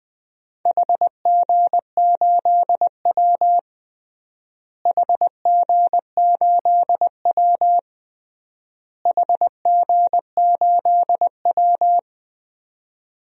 Morse code, heard as a keyed tone, about 21 wpm, 700 Hz. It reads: HG8W HG8W HG8W
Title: DXCC-100 Call Signs - 4 Characters - Repeated 3x - 20wpm